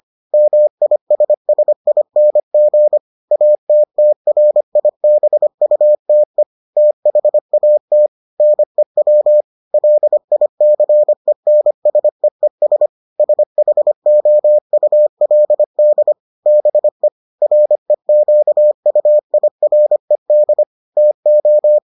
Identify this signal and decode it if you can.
Morse code, heard as a keyed tone, about 25 wpm, 600 Hz. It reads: MISSING ATTRIBUTE THAT NEW LICENSEES SHOULD BE REQUIRED TO